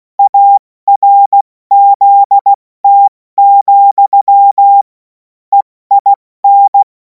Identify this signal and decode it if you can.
Morse code, heard as a keyed tone, about 16 wpm, 800 Hz. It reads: ARZT, EIN
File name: Morsekurs / Witze / Witz 24. Februar 2026 70 F80